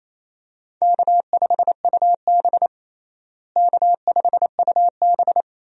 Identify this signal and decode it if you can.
Morse code, heard as a keyed tone, about 27 wpm, 700 Hz. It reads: K5UB K5UB
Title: USA Call Signs - 4 Characters - Repeated 2x - 28wpm